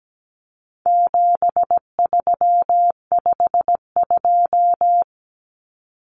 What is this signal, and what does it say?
Morse code, heard as a keyed tone, about 17 wpm, 700 Hz. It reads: 7352
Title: Numbers - 4-Digit 17wpm